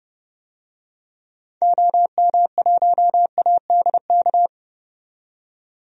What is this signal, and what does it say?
Morse code, heard as a keyed tone, about 30 wpm, 690 Hz. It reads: OM1ADK
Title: DXCC-100 Call Signs - 6 Characters - 30wpm